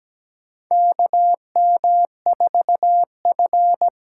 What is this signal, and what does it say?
Morse code, heard as a keyed tone, about 17 wpm, 700 Hz. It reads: KM4F
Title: USA Call Signs - 6 Characters - 8x Character Spacing - 17wpm